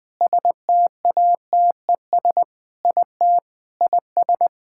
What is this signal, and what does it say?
Morse code, heard as a keyed tone, about 20 wpm, 700 Hz. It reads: STATES IT IS
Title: Sentences with 3-Word Phrases - Spoken Beginning - 20wpm